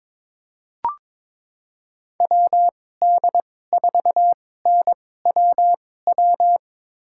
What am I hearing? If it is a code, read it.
Morse code, heard as a keyed tone, about 22 wpm, spaced as for 16 wpm, 700 Hz. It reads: WD4NWW
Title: USA Call Signs - 6 Characters - 2x Character Spacing - 22wpm